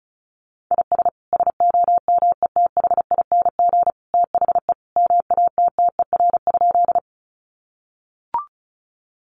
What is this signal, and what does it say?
Morse code, heard as a keyed tone, about 34 wpm, 700 Hz. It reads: IS SOMETHING THE MATTER?
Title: Sentences from Top 700 Words 35wpm